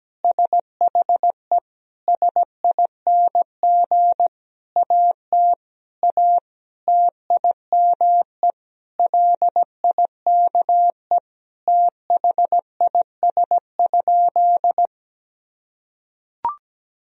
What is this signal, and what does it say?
Morse code, heard as a keyed tone, about 17 wpm, 700 Hz. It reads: SHE SING AT A TIME LIKE THIS?
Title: Sentences from Top 1000 Words - Review Part 2 - 17wpm